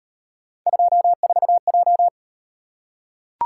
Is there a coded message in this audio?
Yes: Morse code from 0.7 to 2.1 s, heard as a keyed tone, about 37 wpm, 700 Hz. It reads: 24J